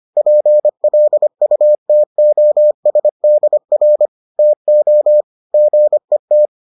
Morse code, heard as a keyed tone, about 25 wpm, 600 Hz. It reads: PLUTOSDR TO GET